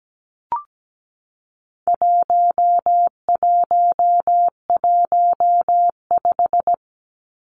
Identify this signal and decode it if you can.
Morse code, heard as a keyed tone, about 17 wpm, 700 Hz. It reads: 1115
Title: Numbers - 4-Digit 17wpm